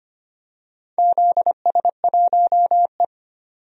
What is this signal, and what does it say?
Morse code, heard as a keyed tone, about 25 wpm, 700 Hz. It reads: ZS1E
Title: DXCC-100 Call Signs - 4 Characters - 25wpm